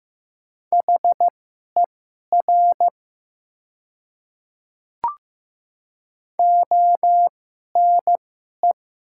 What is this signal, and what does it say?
Morse code, heard as a keyed tone, about 15 wpm, spaced as for 11 wpm, 700 Hz. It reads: HER ONE
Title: 2x Character Spacing - Top 200 Words - 15wpm